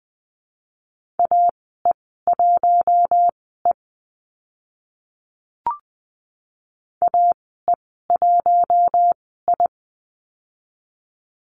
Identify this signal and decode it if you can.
Morse code, heard as a keyed tone, about 20 wpm, spaced as for 14 wpm, 700 Hz. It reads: AE1E AE1I
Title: 042 Lesson 8g - Number 1 - Call Signs 20wpm